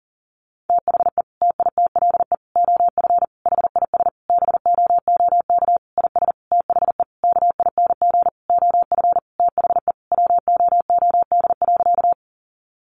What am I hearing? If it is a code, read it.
Morse code, heard as a keyed tone, about 40 wpm, 700 Hz. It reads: THE TITLE OF HIS BOOK IS THE KING OF THE WOOD.
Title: Sentences from Top 1200 Words 40wpm